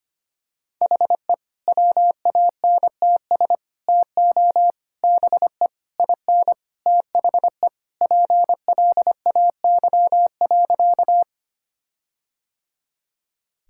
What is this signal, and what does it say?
Morse code, heard as a keyed tone, about 25 wpm, 700 Hz. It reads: HE WANTS TO BE IN THE PLAY.
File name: practice / 25wpm / Sentences with 4-Word Phrases - 25wpm